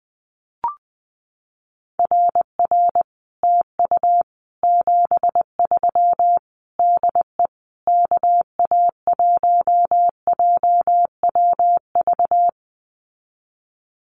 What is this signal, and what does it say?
Morse code, heard as a keyed tone, about 20 wpm, 700 Hz. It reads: RR TU 73 DE KA1JWV
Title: Common QSO Phrases Encore 20wpm